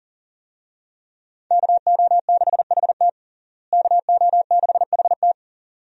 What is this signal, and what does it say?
Morse code, heard as a keyed tone, about 40 wpm, 700 Hz. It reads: KO6HT KO6HT
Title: USA Call Signs - 5 Characters - Repeated 2x - 40wpm